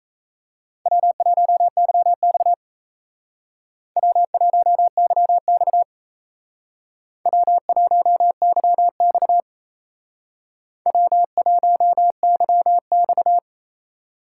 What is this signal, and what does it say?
Morse code, heard as a keyed tone, about 35 wpm, 700 Hz. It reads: W1YX W1YX W1YX W1YX